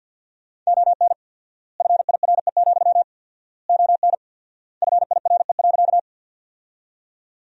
Morse code, heard as a keyed tone, about 49 wpm, 700 Hz. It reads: ON FIRE, ON FIRE.
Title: 4x Word-Spacing - Sentences from Top 400 Words - 50wpm